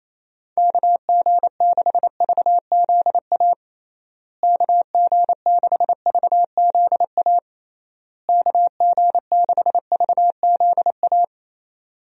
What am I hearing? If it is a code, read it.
Morse code, heard as a keyed tone, about 28 wpm, 700 Hz. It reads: KG6VZA KG6VZA KG6VZA
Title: USA Call Signs - 6 Characters - Repeated 3x - 28wpm